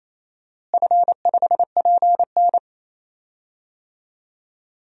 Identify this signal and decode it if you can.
Morse code, heard as a keyed tone, about 28 wpm, 700 Hz. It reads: F5PN
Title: DXCC-100 Call Signs - 4 Characters - Repeated 2x - 28wpm